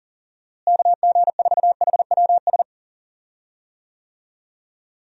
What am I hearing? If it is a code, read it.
Morse code, heard as a keyed tone, about 40 wpm, 700 Hz. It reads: KG4HWS
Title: USA Call Signs - 6 Characters - Repeated 2x - 40wpm